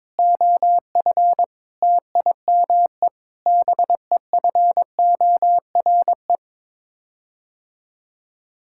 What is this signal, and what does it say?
Morse code, heard as a keyed tone, about 22 wpm, 700 Hz. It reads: OF TIME BEFORE
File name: practice / 22wpm / Sets of 6 Words - 22wpm